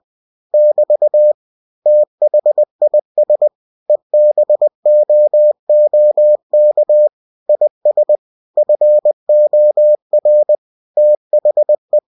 Morse code, heard as a keyed tone, about 20 wpm, 600 Hz. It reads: <BT> THIS EBOOK IS FOR THE
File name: cw-books / winnie-the-pooh / winnie-20wpm-page-0115